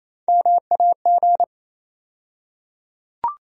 Morse code, heard as a keyed tone, about 28 wpm, 700 Hz. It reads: MAG
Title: USA Call Sign Postfixes - 3 Letters 28wpm